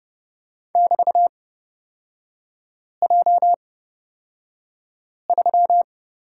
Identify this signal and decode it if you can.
Morse code, heard as a keyed tone, about 31 wpm, spaced as for 5 wpm, 690 Hz. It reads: <BT>J3
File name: Lesson 37e - Letter x - Three Characters 30wpm